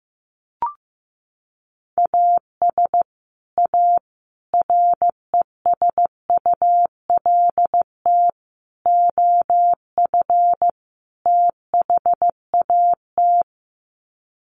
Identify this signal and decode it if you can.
Morse code, heard as a keyed tone, about 15 wpm, 700 Hz. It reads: AS A RESULT OF THAT
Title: Sets of 5 Words Encore 15wpm